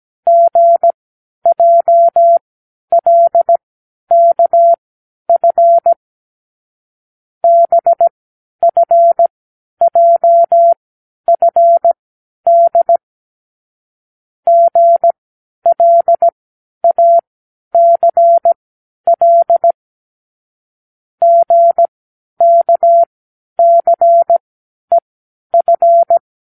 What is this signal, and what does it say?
Morse code, heard as a keyed tone, about 17 wpm, spaced as for 10 wpm, 680 Hz. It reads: GJLKF BFJFD GLACL GKCEF
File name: k7qo / Lesson18(A-L)